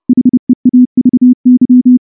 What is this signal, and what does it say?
Morse code, heard as a keyed tone, about 29 wpm, 250 Hz. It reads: HEAVY
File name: morse / morse-training-30WPM